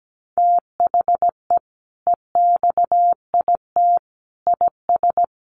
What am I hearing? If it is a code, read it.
Morse code, heard as a keyed tone, about 17 wpm, 700 Hz. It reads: THE EXIT IS
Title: Sentences from Top 1100 Words 17wpm